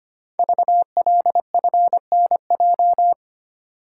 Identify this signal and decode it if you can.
Morse code, heard as a keyed tone, about 25 wpm, 700 Hz. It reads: VLFNJ
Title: Code Groups - 5 Letters - 25wpm